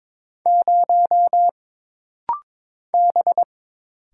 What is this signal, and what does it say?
Morse code, heard as a keyed tone, about 22 wpm, 700 Hz. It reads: 0 B
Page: This rapid-fire timing is geared to push your ICR (Instant Character Recognition) skills.